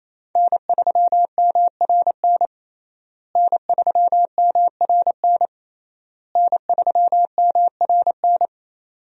Morse code, heard as a keyed tone, about 28 wpm, 700 Hz. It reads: N3MRN N3MRN N3MRN
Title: USA Call Signs - 5 Characters - Repeated 3x - 28wpm